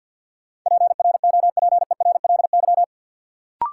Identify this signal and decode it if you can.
Morse code, heard as a keyed tone, about 51 wpm, 700 Hz. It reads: PROPERLY
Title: Sentences from Top 1100 Words - New Word Drill - 50wpm